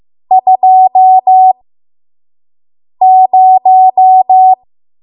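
Morse code, heard as a keyed tone, about 15 wpm, spaced as for 5 wpm, 750 Hz. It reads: 20
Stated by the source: 5 WPM Morse code practice.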